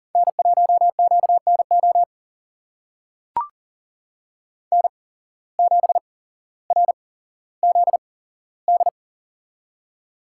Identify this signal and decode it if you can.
Morse code, heard as a keyed tone, about 40 wpm, spaced as for 17 wpm, 700 Hz. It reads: N1QNO N7RZD
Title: USA Call Signs - 5 Characters - 8x Character Spacing - 40wpm